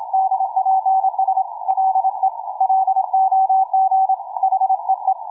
Morse code, heard as a keyed tone, about 26 wpm, 760 Hz, with radio noise. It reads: DPS LE LOG HEE
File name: cw forum / tu2 f5ldy 18mhz 16h46